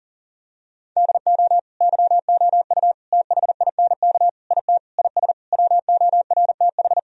Morse code, heard as a keyed tone, about 40 wpm, 700 Hz. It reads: DO YOU THINK IT IS WORTH
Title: Sentences from Top 1000 Words - Review Part 1 - 40wpm